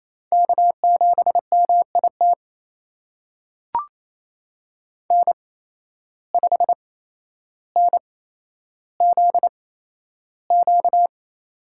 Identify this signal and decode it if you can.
Morse code, heard as a keyed tone, about 28 wpm, spaced as for 12 wpm, 700 Hz. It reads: K7MIT N5NZQ
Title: USA Call Signs - 5 Characters - 8x Character Spacing - 28wpm